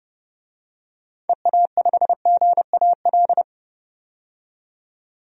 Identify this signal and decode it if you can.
Morse code, heard as a keyed tone, about 30 wpm, 700 Hz. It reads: EA5GAL
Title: DXCC-100 Call Signs - 6 Characters - 30wpm